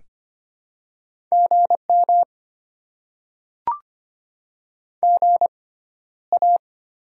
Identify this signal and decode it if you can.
Morse code, heard as a keyed tone, about 25 wpm, spaced as for 13 wpm, 700 Hz. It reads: GM GA
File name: Top 20 QSO Elements - 6x Character Spacing - 25wpm